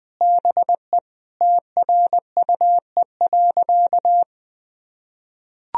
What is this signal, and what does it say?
Morse code, heard as a keyed tone, about 21 wpm, 700 Hz. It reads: BE TRUE.
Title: Sentences from Top 1000 Words - Review Part 4 - 20wpm